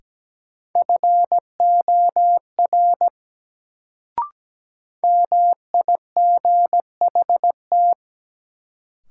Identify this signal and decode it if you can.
Morse code, heard as a keyed tone, about 17 wpm, 700 Hz. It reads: FOR MIGHT
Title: Sets of 2 Words - New Word Drill - 17wpm